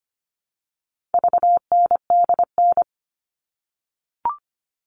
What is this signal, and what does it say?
Morse code, heard as a keyed tone, about 25 wpm, 700 Hz. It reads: VNDN
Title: All Cut Numbers - 4-Digit 25wpm